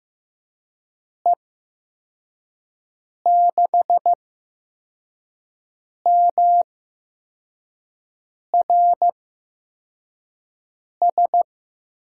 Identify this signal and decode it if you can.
Morse code, heard as a keyed tone, about 15 wpm, spaced as for 4 wpm, 700 Hz. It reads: E6MRS
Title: USA Call Signs - 6 Characters - 8x Character Spacing - 15wpm